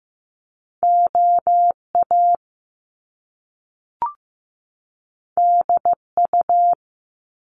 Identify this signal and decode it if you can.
Morse code, heard as a keyed tone, about 15 wpm, 700 Hz. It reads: OA DU
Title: Top Two Letter Combinations in Words 15wpm